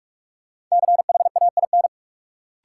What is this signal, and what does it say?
Morse code, heard as a keyed tone, about 46 wpm, 700 Hz. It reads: CHAIN
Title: Sentences from Top 1500 Words - New Word Drill Alt - 45wpm